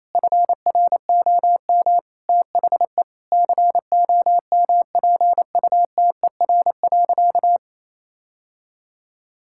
Morse code, heard as a keyed tone, about 28 wpm, 700 Hz. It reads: FROM THE COMPUTER.
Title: Sentences from Top 1000 Words - Review Part 6 - 28wpm